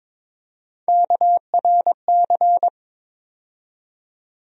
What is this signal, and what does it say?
Morse code, heard as a keyed tone, about 22 wpm, 700 Hz. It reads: KRC
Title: USA Call Sign Postfixes - 3 Letters 22wpm